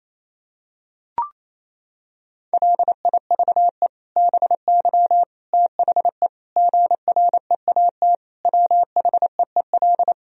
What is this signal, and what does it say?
Morse code, heard as a keyed tone, about 28 wpm, 700 Hz. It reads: LIVE BY THE GREAT WHEEL
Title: Sentences from Top 1000 Words - Review Part 3 - 28wpm